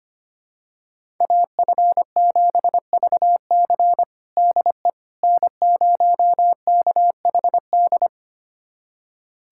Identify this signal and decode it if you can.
Morse code, heard as a keyed tone, about 25 wpm, 700 Hz. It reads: AF7VC DE N0KHD
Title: USA Callsign DE Callsign - 5 Characters - 25wpm